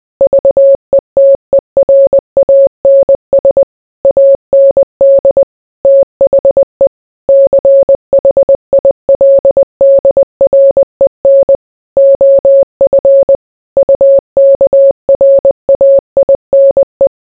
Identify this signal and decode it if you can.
Morse code, heard as a keyed tone, about 20 wpm, 550 Hz. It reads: VETERANS AND THE CHILDREN OF UKRAINE